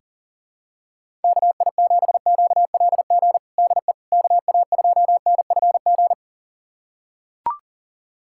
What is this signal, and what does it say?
Morse code, heard as a keyed tone, about 40 wpm, 700 Hz. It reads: KI7QLG DE KA2NFG
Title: USA Callsign DE Callsign - 6 Characters - 40wpm